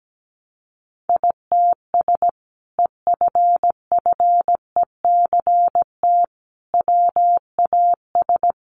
Morse code, heard as a keyed tone, about 17 wpm, 700 Hz. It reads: ITS EFFECT WAS
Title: Sentences from Top 1200 Words 17wpm